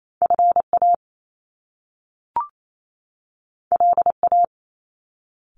Morse code, heard as a keyed tone, about 28 wpm, 700 Hz. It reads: FA LA
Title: Top Two Letter Combinations in Words 28wpm